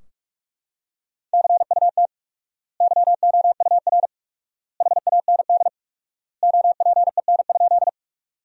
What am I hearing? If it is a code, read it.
Morse code, heard as a keyed tone, about 41 wpm, 700 Hz. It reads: CUT YOUR HAND OPEN?
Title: 4x Word-Spacing - Sentences from Top 300 Words - 45wpm